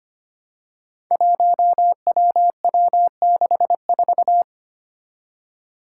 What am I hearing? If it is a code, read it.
Morse code, heard as a keyed tone, about 25 wpm, 700 Hz. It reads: 1WW64
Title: Five Challenging Letter-Number Combinations 25wpm